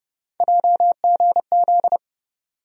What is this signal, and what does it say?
Morse code, heard as a keyed tone, about 30 wpm, 700 Hz. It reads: JGZ
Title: USA Call Sign Postfixes - 3 Letters 30wpm